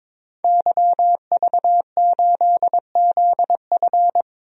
Morse code, heard as a keyed tone, about 22 wpm, 700 Hz. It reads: YV8ZF
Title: DXCC-100 Call Signs - 5 Characters - 22wpm